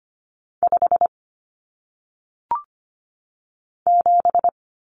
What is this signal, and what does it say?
Morse code, heard as a keyed tone, about 25 wpm, 700 Hz. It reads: 5 7
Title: Common Cut Numbers - 1-Digit 25wpm